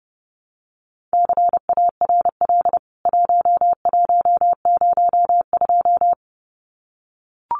Morse code, heard as a keyed tone, about 30 wpm, 700 Hz. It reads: CARL 1102
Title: CWT Exchange 30wpm